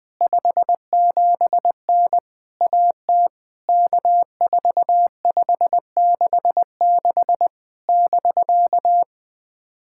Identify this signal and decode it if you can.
Morse code, heard as a keyed tone, about 20 wpm, 700 Hz. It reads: 57N AT K4566 <BK>
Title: POTA QSO Practice - 20wpm